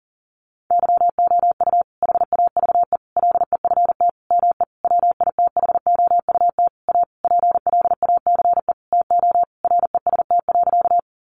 Morse code, heard as a keyed tone, about 40 wpm, 700 Hz. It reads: YOU HAVE LEFT ME WITHOUT A PLACE TO REST.
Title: Sentences from Top 600 Words 40wpm